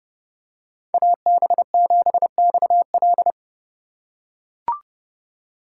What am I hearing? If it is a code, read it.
Morse code, heard as a keyed tone, about 28 wpm, 700 Hz. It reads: AB7XL